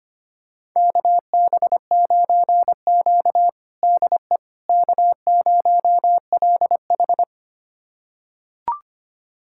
Morse code, heard as a keyed tone, about 25 wpm, 700 Hz. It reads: KB9Q DE K0LH
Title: USA Callsign DE Callsign - 4 Characters - 25wpm